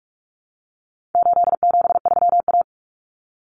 Morse code, heard as a keyed tone, about 44 wpm, 700 Hz. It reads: 873A
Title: Common Cut Numbers - 4-Digit 45wpm